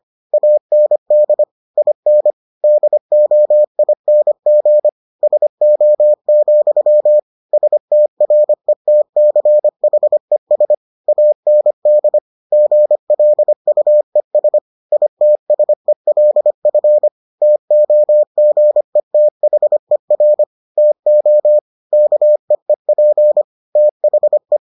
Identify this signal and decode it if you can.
Morse code, heard as a keyed tone, about 25 wpm, 600 Hz. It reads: AND IN DOING SO, STRETCHES AND GLUES ITSELF TOGETHER TO KEEP THE